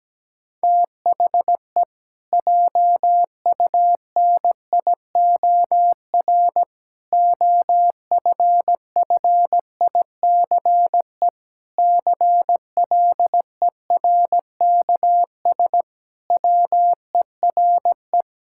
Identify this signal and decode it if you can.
Morse code, heard as a keyed tone, about 17 wpm, 700 Hz. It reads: THE JUNIOR OFFICE CLERKS WERE
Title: Pangram Challenge v2 - 17wpm